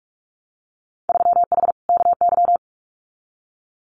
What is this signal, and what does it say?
Morse code, heard as a keyed tone, about 44 wpm, 700 Hz. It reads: 3H KY
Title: Winter Field Day Exchanges Encore 45wpm